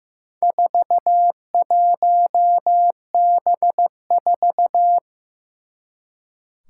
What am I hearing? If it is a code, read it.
Morse code, heard as a keyed tone, about 15 wpm, 700 Hz. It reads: 41B4
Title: Four Challenging Letter-Number Combinations 15wpm